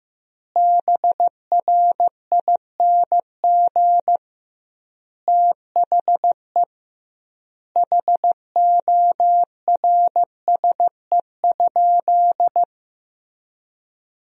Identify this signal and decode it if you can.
Morse code, heard as a keyed tone, about 15 wpm, 700 Hz. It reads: BRING THE HORSE?
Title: 2x Word-Spacing - Sentences from Top 500 Words - 15wpm